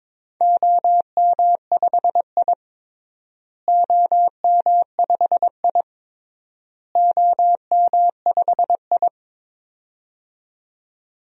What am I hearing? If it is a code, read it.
Morse code, heard as a keyed tone, about 22 wpm, 700 Hz. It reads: OM5I OM5I OM5I
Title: DXCC-100 Call Signs - 4 Characters - Repeated 3x - 22wpm